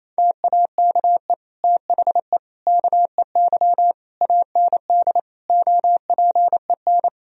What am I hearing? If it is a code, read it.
Morse code, heard as a keyed tone, about 28 wpm, 700 Hz. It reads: TAKE THE KEY AND OPEN
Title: Sentences from Top 1000 Words - Review Part 4 - 28wpm